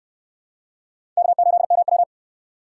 Morse code, heard as a keyed tone, about 65 wpm, 700 Hz. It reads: K8MW